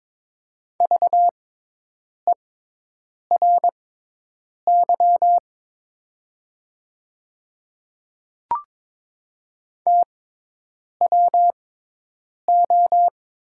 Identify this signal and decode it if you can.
Morse code, heard as a keyed tone, about 22 wpm, spaced as for 8 wpm, 700 Hz. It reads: VERY TWO
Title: 6x Character Spacing - Top 200 Words - 22wpm